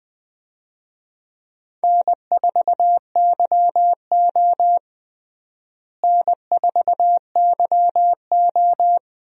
Morse code, heard as a keyed tone, about 20 wpm, 700 Hz. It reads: N4YO N4YO
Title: USA Call Signs - 4 Characters - Repeated 3x - 20wpm